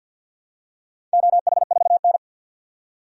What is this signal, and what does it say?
Morse code, heard as a keyed tone, about 51 wpm, 700 Hz. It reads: OH4N
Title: DXCC-100 Call Signs - 4 Characters - 50wpm